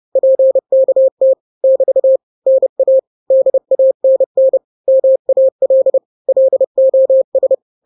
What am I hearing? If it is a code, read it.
Morse code, heard as a keyed tone, about 30 wpm, 530 Hz. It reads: PKT <BT> NA DANN MAL LOS
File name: daily / 29wpm